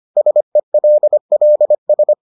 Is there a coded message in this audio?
Yes: Morse code from 0.2 to 2.1 s, heard as a keyed tone, about 25 wpm, 600 Hz. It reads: SELLS